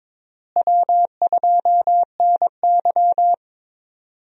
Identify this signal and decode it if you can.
Morse code, heard as a keyed tone, about 22 wpm, 700 Hz. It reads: W2NY
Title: USA Call Signs - 4 Characters - Repeated 3x - 22wpm